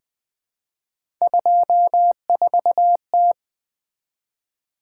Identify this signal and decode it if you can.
Morse code, heard as a keyed tone, about 20 wpm, 700 Hz. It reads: 24T
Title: Three Letter-Number Combinations - 20wpm